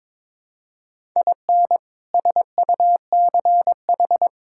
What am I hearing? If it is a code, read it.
Morse code, heard as a keyed tone, about 22 wpm, 700 Hz. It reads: IN SUCH
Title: Sentences from Top 1000 Words - Review Part 5 - 22wpm